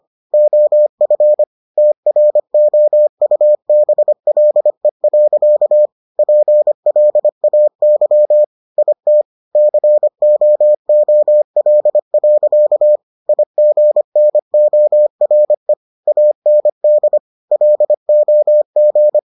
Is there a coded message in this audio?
Yes: Morse code from 0.3 to 19.2 s, heard as a keyed tone, about 25 wpm, 600 Hz. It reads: OF TROUBLE. PLAY IT COOL. IGNORE AND LOG